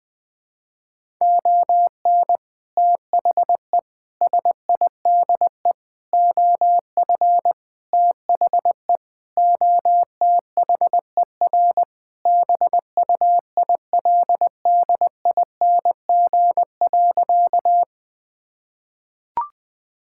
Morse code, heard as a keyed tone, about 21 wpm, 700 Hz. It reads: ON THE SIDE OF THE OTHER BUILDING.
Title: Sentences with 3-Word Phrases - Spoken Beginning - 20wpm